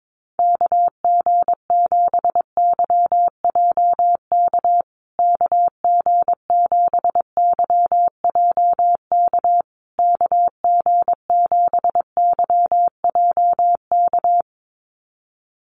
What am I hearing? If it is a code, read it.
Morse code, heard as a keyed tone, about 22 wpm, 690 Hz. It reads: KG7YJK KG7YJK KG7YJK
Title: USA Call Signs Repeated 3x 22wpm